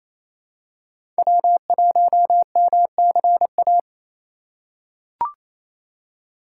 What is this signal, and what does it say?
Morse code, heard as a keyed tone, about 28 wpm, 700 Hz. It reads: W1MCA
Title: USA Call Signs - 5 Characters - 2x Character Spacing - 28wpm